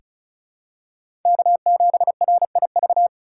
Morse code, heard as a keyed tone, about 36 wpm, 700 Hz. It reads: K7RIV
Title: USA Call Signs - 5 Characters - 2x Character Spacing - 35wpm